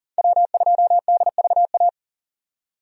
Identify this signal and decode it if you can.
Morse code, heard as a keyed tone, about 40 wpm, 700 Hz. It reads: W2DVA
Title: USA Call Signs - 5 Characters - 40wpm